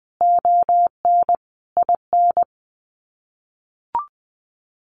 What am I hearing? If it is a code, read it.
Morse code, heard as a keyed tone, about 19 wpm, 700 Hz. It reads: ON IN
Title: Sets of 5 Words Encore 20wpm